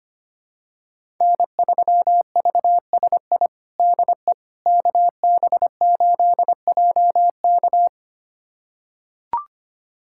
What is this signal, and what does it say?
Morse code, heard as a keyed tone, about 25 wpm, 700 Hz. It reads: N3VSI DE KB8JK
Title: USA Callsign DE Callsign - 5 Characters - 25wpm